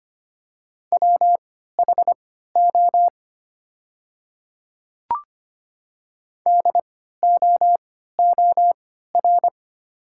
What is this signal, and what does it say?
Morse code, heard as a keyed tone, about 25 wpm, spaced as for 14 wpm, 700 Hz. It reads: WHO DOOR
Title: 3x Character Spacing - Top 300 Words - 25wpm